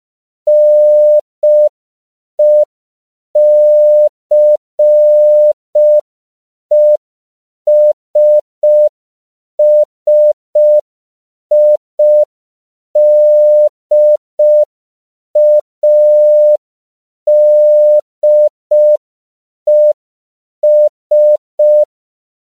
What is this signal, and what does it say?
Morse code, heard as a keyed tone, about 5 wpm, 600 Hz. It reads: NECESSIDADES
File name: CW / 5PPM